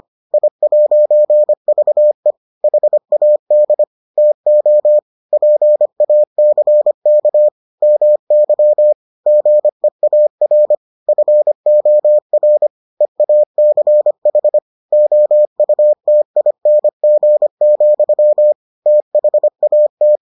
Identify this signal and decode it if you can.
Morse code, heard as a keyed tone, about 25 wpm, 600 Hz. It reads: I'VE HAD TO PACK MY GEAR FOR EACH OUTING, THAT